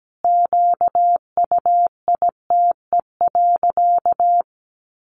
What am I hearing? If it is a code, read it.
Morse code, heard as a keyed tone, about 17 wpm, 700 Hz. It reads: QUITE.
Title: Sentences from Top 1300 Words 17wpm